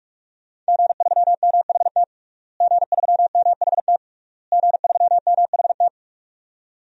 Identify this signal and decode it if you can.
Morse code, heard as a keyed tone, about 46 wpm, 700 Hz. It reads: G3MHT G3MHT G3MHT
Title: DXCC-100 Call Signs - 5 Characters - Repeated 3x - 45wpm